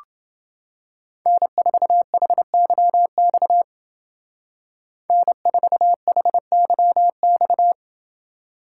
Morse code, heard as a keyed tone, about 28 wpm, 700 Hz. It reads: N4HYX N4HYX